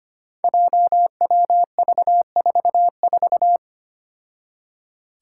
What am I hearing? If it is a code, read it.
Morse code, heard as a keyed tone, about 25 wpm, 700 Hz. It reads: JWV44
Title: Five Challenging Letter-Number Combinations 25wpm